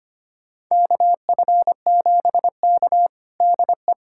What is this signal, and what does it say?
Morse code, heard as a keyed tone, about 25 wpm, 700 Hz. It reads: KF7K DE
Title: USA Callsign DE Callsign - 4 Characters - 25wpm